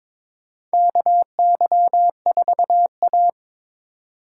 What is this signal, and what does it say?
Morse code, heard as a keyed tone, about 22 wpm, 700 Hz. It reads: KY4A
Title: USA Call Signs - 3x Character Spacing - 22wpm